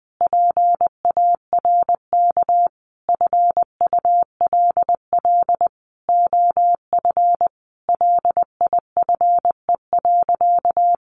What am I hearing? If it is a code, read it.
Morse code, heard as a keyed tone, about 20 wpm, 700 Hz. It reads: PARK FULL OF LIFE.
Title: Sentences from Top 1300 Words 20wpm